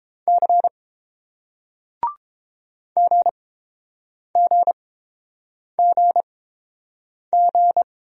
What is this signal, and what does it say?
Morse code, heard as a keyed tone, about 29 wpm, spaced as for 7 wpm, 700 Hz. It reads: C GGGG